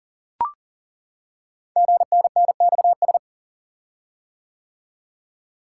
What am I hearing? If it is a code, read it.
Morse code, heard as a keyed tone, about 41 wpm, 690 Hz. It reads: GNNXS
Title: Code Groups - 5 Letters - 40wpm